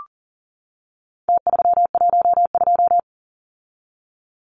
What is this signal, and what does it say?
Morse code, heard as a keyed tone, about 40 wpm, 700 Hz. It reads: T312
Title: Numbers - Contest Style - 4-Digit 40wpm